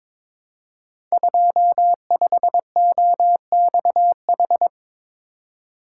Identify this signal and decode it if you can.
Morse code, heard as a keyed tone, about 22 wpm, 700 Hz. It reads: 25OXH
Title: Code Groups - 5 Letter:Numbers - 22wpm